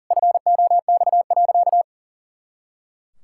Morse code, heard as a keyed tone, about 40 wpm, 700 Hz. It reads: FOX.
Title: Pangram Challenge v2 - 40wpm